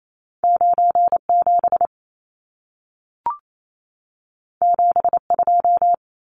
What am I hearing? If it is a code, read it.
Morse code, heard as a keyed tone, about 28 wpm, 700 Hz. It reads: 97 72
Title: Numbers - 2-Digit 28wpm